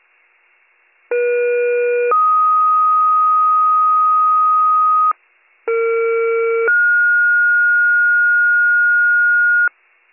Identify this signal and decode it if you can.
Two-tone page, heard as a one-second tone, then a three-second tone, about 480 then 1250 Hz, about 460 then 1430 Hz, with noise. It is heard from 1.1 to 9.7 s.